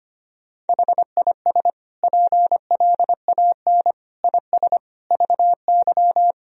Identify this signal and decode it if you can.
Morse code, heard as a keyed tone, about 25 wpm, 700 Hz. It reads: HIS PLAN IS VY
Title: Sentences from Top 500 Words and Abbreviations 25wpm